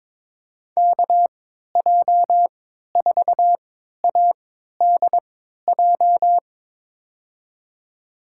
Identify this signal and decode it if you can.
Morse code, heard as a keyed tone, about 22 wpm, spaced as for 12 wpm, 700 Hz. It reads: KJ4ADJ
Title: USA Call Signs - 6 Characters - 3x Character Spacing - 22wpm